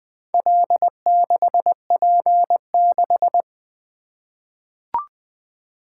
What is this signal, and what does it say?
Morse code, heard as a keyed tone, about 20 wpm, 700 Hz. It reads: L6P6
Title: Four Letter-Number Combinations - 20wpm